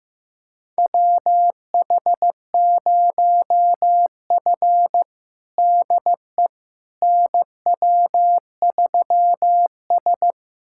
Morse code, heard as a keyed tone, about 16 wpm, 700 Hz. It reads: WH0F DE NW3S
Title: USA Callsign DE Callsign - 4 Characters - 15wpm